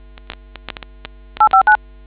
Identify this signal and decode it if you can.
DTMF, heard as telephone touch tones, tones of about 110 ms. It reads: 859